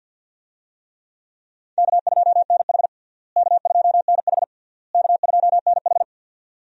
Morse code, heard as a keyed tone, about 51 wpm, 700 Hz. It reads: K2NH K2NH K2NH
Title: USA Call Signs - 4 Characters - Repeated 3x - 50wpm